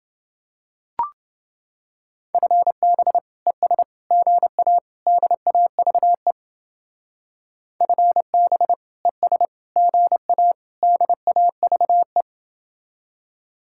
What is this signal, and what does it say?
Morse code, heard as a keyed tone, about 28 wpm, 700 Hz. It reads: FB ES GA DAVE FB ES GA DAVE